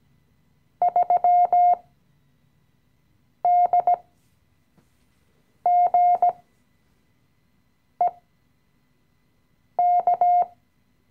Morse code, heard as a keyed tone, about 17 wpm, spaced as for 5 wpm, 710 Hz. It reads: 3DGEK
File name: CW-Online-Kurs / Anfänger / Lektion 017